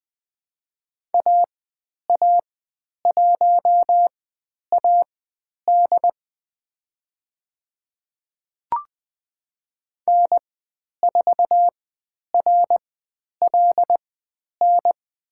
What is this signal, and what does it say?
Morse code, heard as a keyed tone, about 20 wpm, spaced as for 10 wpm, 700 Hz. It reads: AA1AD N4RLN
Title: 109 Lesson 18g - Letter m - Call Signs 20wpm